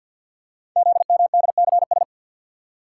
Morse code, heard as a keyed tone, about 48 wpm, 700 Hz. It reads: ZMDCS
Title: Code Groups - 5 Letters - 50wpm